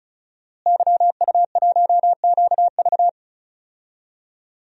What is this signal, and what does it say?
Morse code, heard as a keyed tone, about 36 wpm, 700 Hz. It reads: YU1QV
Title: DXCC-100 Call Signs - 5 Characters - Repeated 2x - 35wpm